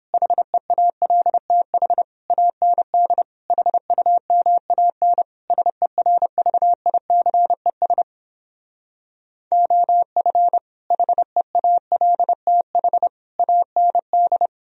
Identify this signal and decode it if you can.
Morse code, heard as a keyed tone, about 29 wpm, 700 Hz. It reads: HEALTH AND HUMAN SERVICES OF HEALTH AND